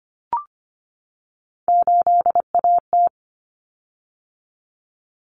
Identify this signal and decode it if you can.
Morse code, heard as a keyed tone, about 25 wpm, 700 Hz. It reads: 8AT
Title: Common Cut Numbers - 3-Digit 25wpm